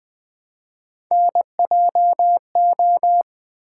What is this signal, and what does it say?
Morse code, heard as a keyed tone, about 20 wpm, 700 Hz. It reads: NJO
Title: Three Letter-Number Combinations - 20wpm